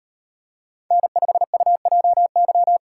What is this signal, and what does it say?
Morse code, heard as a keyed tone, about 37 wpm, 700 Hz. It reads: N5UJY